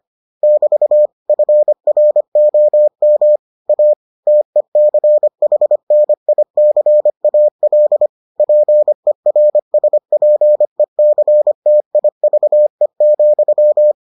Morse code, heard as a keyed tone, about 25 wpm, 600 Hz. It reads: <BT> FROM A TECHNICAL PERSPECTIVE,